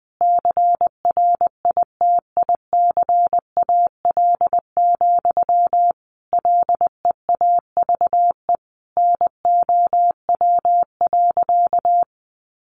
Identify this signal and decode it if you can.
Morse code, heard as a keyed tone, about 20 wpm, 700 Hz. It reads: CRITICAL, LEAVE NOW.
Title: Sentences from Top 1100 Words 20wpm